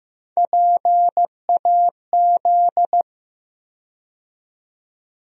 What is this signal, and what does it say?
Morse code, heard as a keyed tone, about 15 wpm, 700 Hz. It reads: PAZ
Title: USA Call Sign Postfixes - 3 Letters 15wpm